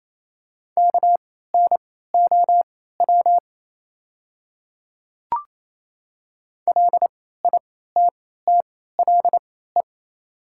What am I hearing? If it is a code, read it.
Morse code, heard as a keyed tone, about 28 wpm, spaced as for 16 wpm, 700 Hz. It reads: KNOW LITTLE
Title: 3x Character Spacing - Top 200 Words - 28wpm